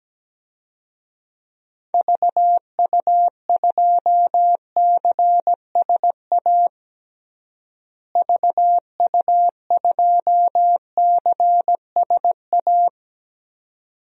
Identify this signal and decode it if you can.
Morse code, heard as a keyed tone, about 17 wpm, 700 Hz. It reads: VU2CSA VU2CSA
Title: DXCC-100 Call Signs - 6 Characters - Repeated 2x - 17wpm